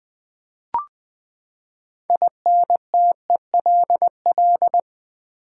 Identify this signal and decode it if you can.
Morse code, heard as a keyed tone, about 20 wpm, 700 Hz. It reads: INTELL
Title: Build A Word - 6 Letters At A Time - 12-Letter Words - 20wpm